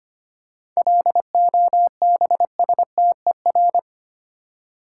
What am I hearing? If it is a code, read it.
Morse code, heard as a keyed tone, about 25 wpm, 690 Hz. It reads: LOBSTER
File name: practice / 25wpm / Easy Words with Hints - 25wpm